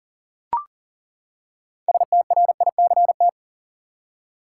Morse code, heard as a keyed tone, about 40 wpm, 700 Hz. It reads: STRICT
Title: Six Letter Words - 40wpm